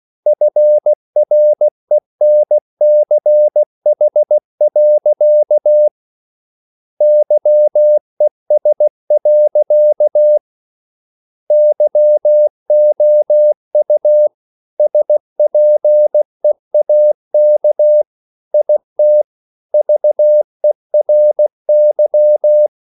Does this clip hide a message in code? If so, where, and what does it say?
Yes: Morse code from 0.3 to 22.7 s, heard as a keyed tone, about 16 wpm, 600 Hz. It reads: FRENCH. YES. YOU SPEAK IT VERY